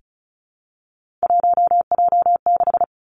Morse code, heard as a keyed tone, about 36 wpm, 700 Hz. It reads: 1J6
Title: Three Challenging Letter-Number Combinations 35wpm